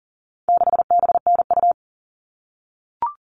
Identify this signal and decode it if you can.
Morse code, heard as a keyed tone, about 40 wpm, 700 Hz. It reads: 6BNU
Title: All Cut Numbers - 4-Digit 40wpm